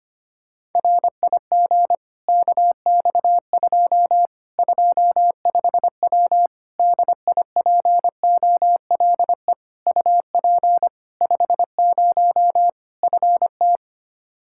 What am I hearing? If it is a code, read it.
Morse code, heard as a keyed tone, about 25 wpm, 700 Hz. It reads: RIG KX2 25W DIPOLE UP 50 FT
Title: Practice QSO - Exchange 2 - Info - 25wpm